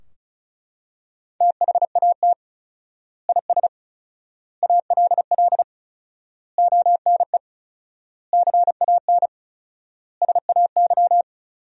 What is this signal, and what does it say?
Morse code, heard as a keyed tone, about 36 wpm, 700 Hz. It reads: THAT IS ALL ONE CAN SAY
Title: 4x Word-Spacing - Sentences from Top 200 Words - 35wpm